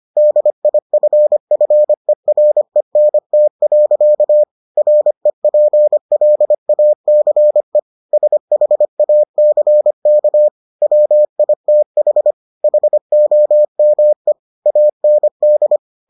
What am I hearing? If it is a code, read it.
Morse code, heard as a keyed tone, about 25 wpm, 600 Hz. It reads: DIFFERENT. REPLACE SHACK WITH HOME AND